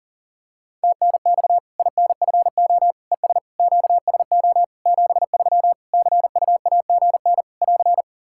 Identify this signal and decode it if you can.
Morse code, heard as a keyed tone, about 40 wpm, 700 Hz. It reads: TNX INFO ES QSO 73 CUAGN <AR>
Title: Practice QSO - Exchange 4 - Ending - 40wpm